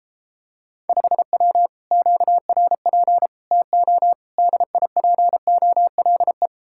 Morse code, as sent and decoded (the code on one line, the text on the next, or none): ..... .-- / --.- .-. .--. / - --- / -.. .. .--. --- .-.. .
5W QRP TO DIPOLE